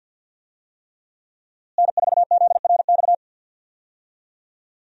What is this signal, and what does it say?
Morse code, heard as a keyed tone, about 53 wpm, 700 Hz. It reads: N4ZRX
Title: USA Call Signs - 5 Characters - Repeated 3x - 50wpm